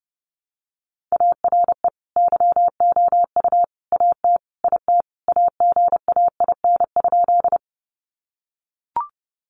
Morse code, heard as a keyed tone, about 30 wpm, 700 Hz. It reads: ARE YOU AT IT AGAIN?
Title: Sentences from Top 200 Words 30wpm